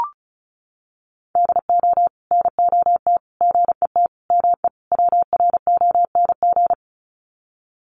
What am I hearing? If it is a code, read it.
Morse code, heard as a keyed tone, about 34 wpm, 700 Hz. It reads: DO NOT GET ME WRONG
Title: Sets of 5 Words Encore 35wpm